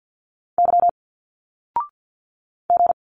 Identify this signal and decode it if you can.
Morse code, heard as a keyed tone, about 51 wpm, 700 Hz. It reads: Y G
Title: Single Letters 50wpm